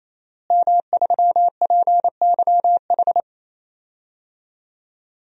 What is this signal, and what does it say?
Morse code, heard as a keyed tone, about 28 wpm, 700 Hz. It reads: M3PYH
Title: DXCC-100 Call Signs - 5 Characters - Repeated 3x - 28wpm